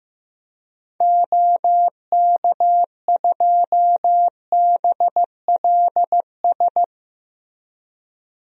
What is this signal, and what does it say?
Morse code, heard as a keyed tone, about 15 wpm, 700 Hz. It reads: OK2BLS
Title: DXCC-100 Call Signs - 6 Characters - Repeated 2x - 15wpm